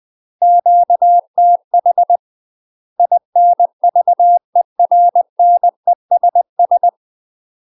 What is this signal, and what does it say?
Morse code, heard as a keyed tone, about 20 wpm, 700 Hz. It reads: QTH INVERNESS
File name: UK MASTER QSOs TRAINING / 20WPM / UK MASTER QSOs 0004